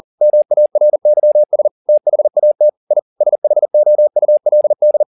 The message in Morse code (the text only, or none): MARYS THAT I SHOULD